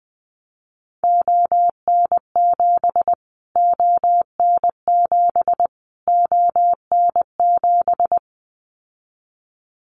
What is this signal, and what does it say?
Morse code, heard as a keyed tone, about 20 wpm, 700 Hz. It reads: ON7 ON7 ON7
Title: Callsign Prefixes for Top 25 DXCC - Repeated 3x - 20wpm